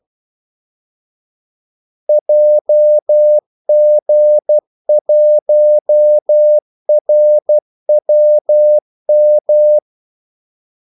Morse code, heard as a keyed tone, about 12 wpm, 600 Hz. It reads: JG1RWM
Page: 2 ekstra ordmellemrum mellem kaldesignalerne